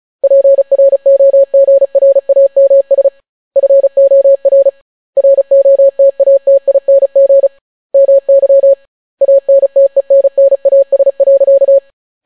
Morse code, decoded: PROGRAMS FOR ROTATING MY ANTENNAS.

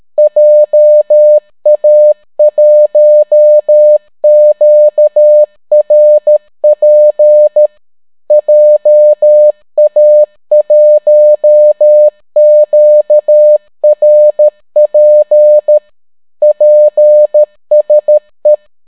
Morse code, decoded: JA1QRP JA1QRP PSE

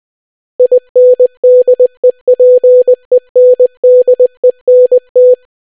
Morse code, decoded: INDEPENDENT

